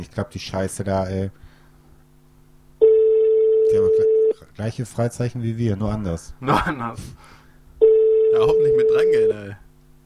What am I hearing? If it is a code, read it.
Ringback tone (France), 440 Hz, with speech.